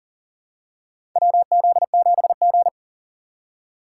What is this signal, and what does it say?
Morse code, heard as a keyed tone, about 40 wpm, 700 Hz. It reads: WZ7G
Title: USA Call Signs - 4x Character Spacing - 40wpm